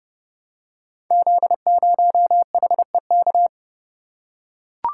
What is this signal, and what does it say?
Morse code, heard as a keyed tone, about 30 wpm, 700 Hz. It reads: Z0HEK
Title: Code Groups - 5 Letter:Numbers - 30wpm